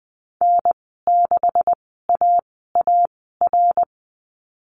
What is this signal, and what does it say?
Morse code, heard as a keyed tone, about 19 wpm, spaced as for 14 wpm, 700 Hz. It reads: N6AAR
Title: 173 Lesson 28g - Number 9 - Call Signs 20wpm